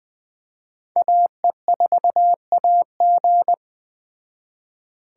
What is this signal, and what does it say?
Morse code, heard as a keyed tone, about 19 wpm, 700 Hz. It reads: AE4AG